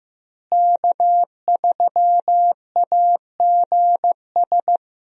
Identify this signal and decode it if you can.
Morse code, heard as a keyed tone, about 15 wpm, 700 Hz. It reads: K3AGS
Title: USA Call Signs - 5 Characters - Repeated 3x - 15wpm